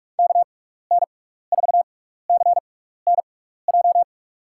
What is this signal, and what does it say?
Morse code, heard as a keyed tone, about 46 wpm, spaced as for 16 wpm, 700 Hz. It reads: KN4CNJ
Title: USA Call Signs - 6 Characters - 6x Character Spacing - 45wpm